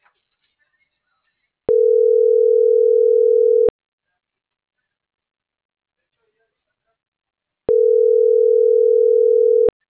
Ringback tone (North America), 440 + 480 Hz.